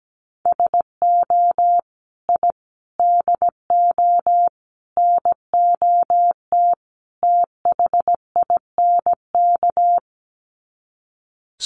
Morse code, heard as a keyed tone, about 17 wpm, 700 Hz. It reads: SO I DO NOT THINK
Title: Sets of 5 Words 17wpm